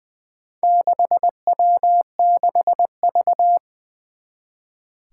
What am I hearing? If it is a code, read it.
Morse code, heard as a keyed tone, about 20 wpm, 700 Hz. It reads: 6W6V
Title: Four Challenging Letter-Number Combinations 20wpm